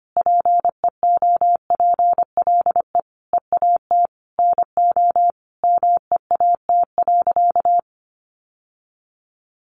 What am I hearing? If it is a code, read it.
Morse code, heard as a keyed tone, about 25 wpm, 700 Hz. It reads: PEOPLE EAT NO MEAT.
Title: Sentences from Top 1300 Words 25wpm